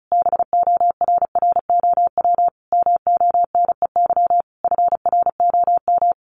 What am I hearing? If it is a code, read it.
Morse code, heard as a keyed tone, about 34 wpm, 700 Hz. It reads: BORROW MONEY FROM
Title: Sentences from Top 1500 Words 35wpm